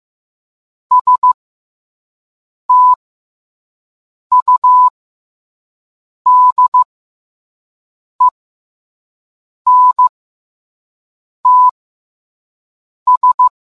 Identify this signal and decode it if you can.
Morse code, heard as a keyed tone, about 15 wpm, spaced as for 5 wpm, 1000 Hz. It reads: STUDENTS